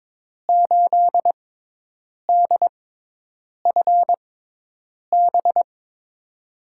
Morse code, heard as a keyed tone, about 22 wpm, spaced as for 8 wpm, 700 Hz. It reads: 8DFB
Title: USA Call Signs - 6 Characters - 6x Character Spacing - 22wpm